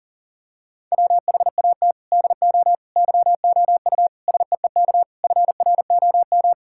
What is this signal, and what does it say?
Morse code, heard as a keyed tone, about 40 wpm, 700 Hz. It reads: WHAT DO YOU SEEK FROM
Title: Sentences from Top 1600 Words 40wpm